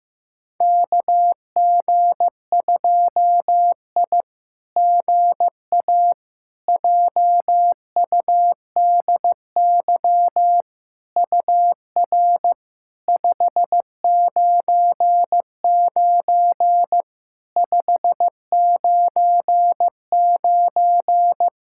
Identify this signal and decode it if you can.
Morse code, heard as a keyed tone, about 16 wpm, 700 Hz. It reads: KG2I GA JUDY UR 599 599
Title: SOTA QSO Practice 15wpm